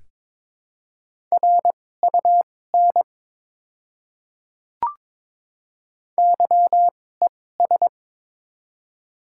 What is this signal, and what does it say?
Morse code, heard as a keyed tone, about 22 wpm, spaced as for 16 wpm, 700 Hz. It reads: RUN YES
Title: 2x Character Spacing - Top 500 Words - 22wpm